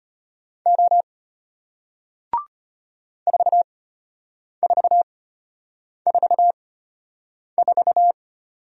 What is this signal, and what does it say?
Morse code, heard as a keyed tone, about 34 wpm, spaced as for 8 wpm, 700 Hz. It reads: O 4444